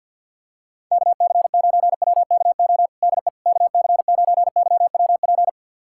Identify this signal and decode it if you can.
Morse code, heard as a keyed tone, about 51 wpm, 700 Hz. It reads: KC9WKO DE KC9YWP
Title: USA Callsign DE Callsign 50wpm